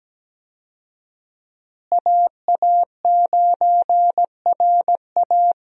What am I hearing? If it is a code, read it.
Morse code, heard as a keyed tone, about 17 wpm, 700 Hz. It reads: AA9RA
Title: USA Call Signs - 5 Characters - Repeated 2x - 17wpm